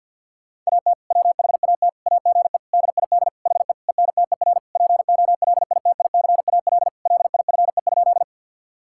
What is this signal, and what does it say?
Morse code, heard as a keyed tone, about 50 wpm, 700 Hz. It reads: AT WHAT AGE DID HE ENTER POLITICAL LIFE?
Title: Sentences from Top 1000 Words - Review Part 5 - 50wpm